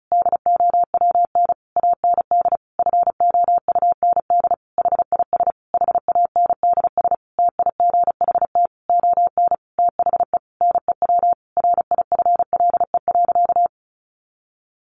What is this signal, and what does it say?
Morse code, heard as a keyed tone, about 34 wpm, 700 Hz. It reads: DOWN AND FOUND HIS HANDS TIGHT ON THE NEW RIFLE.
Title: A Sound Of Thunder 35wpm